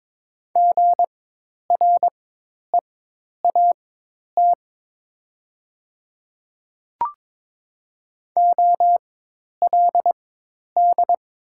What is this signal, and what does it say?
Morse code, heard as a keyed tone, about 22 wpm, spaced as for 10 wpm, 690 Hz. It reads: GREAT OLD
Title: 4x Character Spacing - Top 200 Words - 22wpm